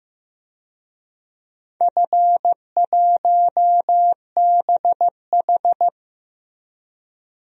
Morse code, heard as a keyed tone, about 15 wpm, 700 Hz. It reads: F1BH
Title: DXCC-100 Call Signs - 4 Characters - 15wpm